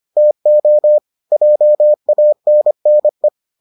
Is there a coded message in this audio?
Yes: Morse code from 0.2 to 3.3 s, heard as a keyed tone, about 25 wpm, 600 Hz. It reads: TO JANNE